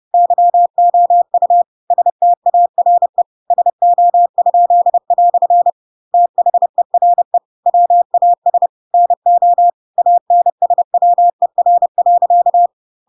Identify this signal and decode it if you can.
Morse code, heard as a keyed tone, about 30 wpm, 700 Hz. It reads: YOU STARE SO?" THERE WAS NO ANSWER.
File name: Pinocchio / Pinocchio ch3 30 30 ik1hge